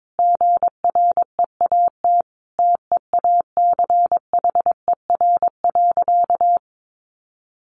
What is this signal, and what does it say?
Morse code, heard as a keyed tone, about 22 wpm, 700 Hz. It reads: GREAT TEACHER.
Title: Sentences from Top 900 Words 22wpm